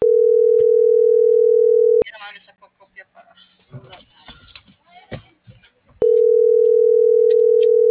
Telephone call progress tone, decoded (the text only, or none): ringback tone